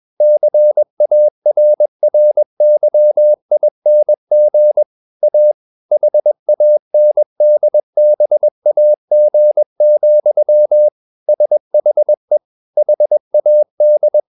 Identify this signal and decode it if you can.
Morse code, heard as a keyed tone, about 21 wpm, 600 Hz. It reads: CARRYING A HANDBAG, SHE HAD
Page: CW med Gnister